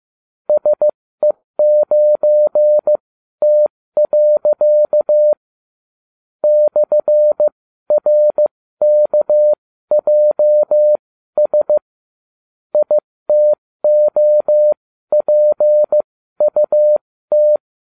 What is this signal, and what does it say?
Morse code, heard as a keyed tone, about 15 wpm, spaced as for 13 wpm, 600 Hz. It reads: SE9T. /RKJS ITOPUT